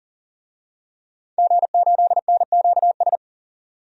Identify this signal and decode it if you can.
Morse code, heard as a keyed tone, about 40 wpm, 700 Hz. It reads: G8NQS
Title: DXCC-100 Call Signs - 5 Characters - 40wpm